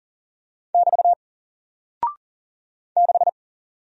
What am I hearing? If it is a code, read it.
Morse code, heard as a keyed tone, about 40 wpm, 700 Hz. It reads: <BT> 6
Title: Single Character (Letter:Number:Punctuation:Prosign) Alt 40wpm